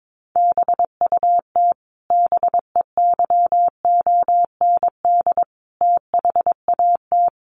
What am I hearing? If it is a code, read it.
Morse code, heard as a keyed tone, about 22 wpm, 700 Hz. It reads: BUT BEYOND THAT
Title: Sentences from Top 1300 Words 22wpm